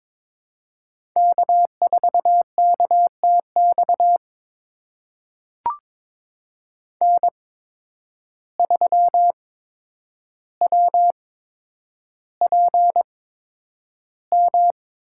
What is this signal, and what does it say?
Morse code, heard as a keyed tone, about 22 wpm, spaced as for 9 wpm, 700 Hz. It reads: K4KTX N3WPM
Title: USA Call Signs - 5 Characters - 8x Character Spacing - 22wpm